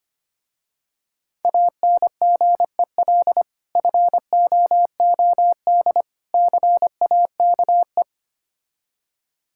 Morse code, heard as a keyed tone, about 25 wpm, 700 Hz. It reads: ANGEL FOOD CAKE
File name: practice / 25wpm / Difficult Words with Hints - 25wpm